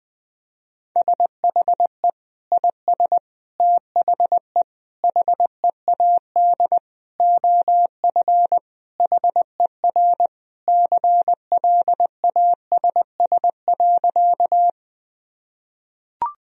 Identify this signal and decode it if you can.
Morse code, heard as a keyed tone, about 20 wpm, 700 Hz. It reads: SHE IS THE HEAD OF HER CLASS.
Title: Sentences from Top 1000 Words - Review Part 2 - 20wpm